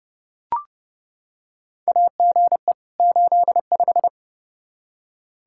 Morse code, heard as a keyed tone, about 32 wpm, 700 Hz. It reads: AGE 85
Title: Practice QSO - Exchange 3 - More Info - 30wpm